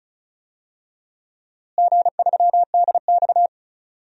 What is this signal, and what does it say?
Morse code, heard as a keyed tone, about 34 wpm, 700 Hz. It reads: G3DX
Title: DXCC-100 Call Signs - 4 Characters - Repeated 3x - 35wpm